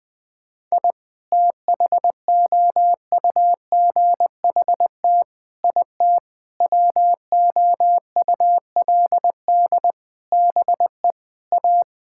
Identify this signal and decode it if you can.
Morse code, heard as a keyed tone, about 20 wpm, 700 Hz. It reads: I THOUGHT IT WOULD BE A
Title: Sentences with 9-Word Phrases - Spoken Beginning - 20wpm